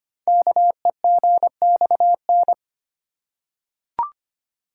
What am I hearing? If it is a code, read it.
Morse code, heard as a keyed tone, about 25 wpm, 700 Hz. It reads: KEGXN
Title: Code Groups - 5 Letters - 25wpm